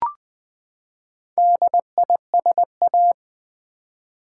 Morse code, heard as a keyed tone, about 20 wpm, 700 Hz. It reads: DISA
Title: Build A Word - 4 Letters At A Time - 12-Letter Words - 20wpm